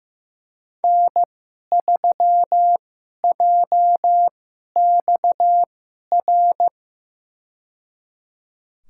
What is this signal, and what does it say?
Morse code, heard as a keyed tone, about 15 wpm, spaced as for 11 wpm, 700 Hz. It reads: N3JXR
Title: USA Call Signs - 5 Characters - 2x Character Spacing - 15wpm